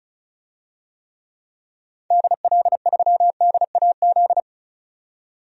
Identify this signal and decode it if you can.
Morse code, heard as a keyed tone, about 36 wpm, 700 Hz. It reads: DL3DAZ
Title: DXCC-100 Call Signs - 6 Characters - 35wpm